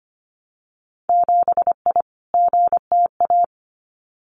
Morse code, heard as a keyed tone, about 25 wpm, 690 Hz. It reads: 7I GTA
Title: Winter Field Day Exchanges Encore 25wpm